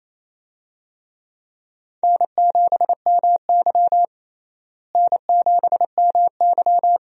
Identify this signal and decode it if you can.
Morse code, heard as a keyed tone, about 28 wpm, 700 Hz. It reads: N7MY N7MY
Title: USA Call Signs - 4 Characters - Repeated 2x - 28wpm